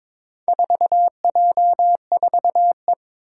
Morse code, heard as a keyed tone, about 21 wpm, 700 Hz. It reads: 4J4E
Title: Four Challenging Letter-Number Combinations 22wpm